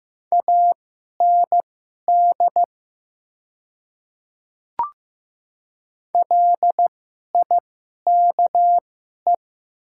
Morse code, heard as a keyed tone, about 15 wpm, spaced as for 11 wpm, 690 Hz. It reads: AND LIKE
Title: 2x Character Spacing - Top 100 Words - 15wpm